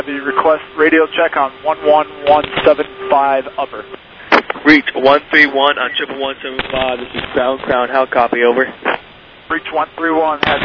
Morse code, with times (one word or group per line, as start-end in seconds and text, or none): none